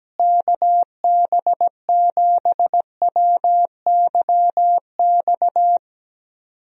Morse code, heard as a keyed tone, about 17 wpm, 700 Hz. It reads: KB7WYX
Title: USA Call Signs - 6 Characters - Repeated 2x - 17wpm